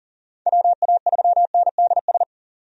Morse code, heard as a keyed tone, about 40 wpm, 700 Hz. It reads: WA3NDS
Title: USA Call Signs - 6 Characters - 40wpm